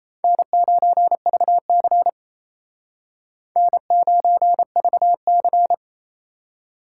Morse code, heard as a keyed tone, about 37 wpm, 700 Hz. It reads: N9VC N9VC